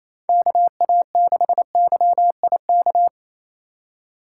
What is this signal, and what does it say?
Morse code, heard as a keyed tone, about 34 wpm, 700 Hz. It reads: KA6YIK
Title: Speed Racing - USA 2x3 Call Signs 28wpm